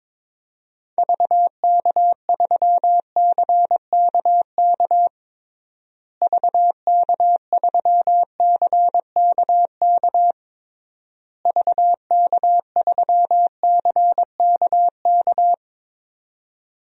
Morse code, heard as a keyed tone, about 21 wpm, 700 Hz. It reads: VK3CKK VK3CKK VK3CKK
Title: DXCC-100 Call Signs - 6 Characters - Repeated 3x - 22wpm